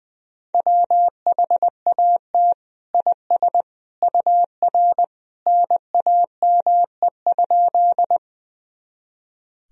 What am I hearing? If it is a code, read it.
Morse code, heard as a keyed tone, about 21 wpm, 700 Hz. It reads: WHAT IS UR NAME?
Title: Sentences from Top 200 Words and Abbreviations 20wpm